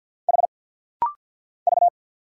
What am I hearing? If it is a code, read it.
Morse code, heard as a keyed tone, about 50 wpm, 700 Hz. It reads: H V
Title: Single Letter-Number - Rapid-Fire 50wpm